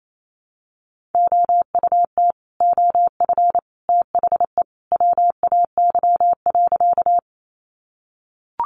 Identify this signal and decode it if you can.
Morse code, heard as a keyed tone, about 28 wpm, 700 Hz. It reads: OUT OF THE WAY.
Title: Sentences from Top 100 Words 28wpm